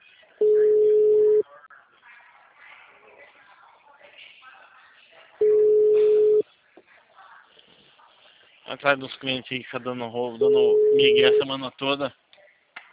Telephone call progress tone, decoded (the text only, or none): ringback tone